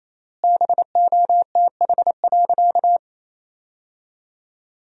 Morse code, heard as a keyed tone, about 28 wpm, 700 Hz. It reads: BOTH.
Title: 4x Word-Spacing - Sentences from Top 300 Words - 28wpm